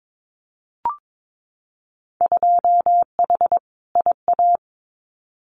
Morse code, heard as a keyed tone, about 22 wpm, 700 Hz. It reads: 2H IA
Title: Winter Field Day Exchanges Encore 22wpm